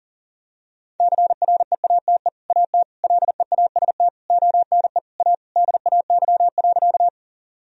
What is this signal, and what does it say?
Morse code, heard as a keyed tone, about 40 wpm, 700 Hz. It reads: CREATE AT LEAST ONE A DAY.
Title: Sentences from Top 1000 Words - Review Part 3 - 40wpm